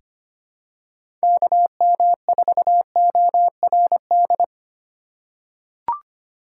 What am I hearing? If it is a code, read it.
Morse code, heard as a keyed tone, about 25 wpm, 700 Hz. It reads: KM4ORD
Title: USA Call Signs - 8x Character Spacing - 25wpm